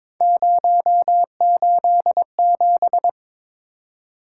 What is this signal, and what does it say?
Morse code, heard as a keyed tone, about 22 wpm, 700 Hz. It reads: 087
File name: practice / 22wpm / Three Letter-Number Combinations - 22wpm